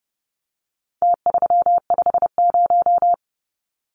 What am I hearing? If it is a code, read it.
Morse code, heard as a keyed tone, about 30 wpm, 700 Hz. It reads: T350
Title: Numbers - Contest Style - 4-Digit 30wpm